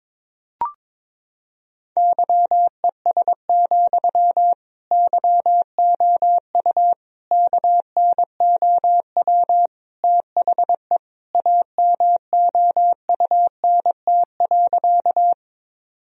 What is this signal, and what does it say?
Morse code, heard as a keyed tone, about 22 wpm, 700 Hz. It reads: YES, YOU KNOW THE AMOUNT.
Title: Sentences from Top 1000 Words - Review Part 4 - 22wpm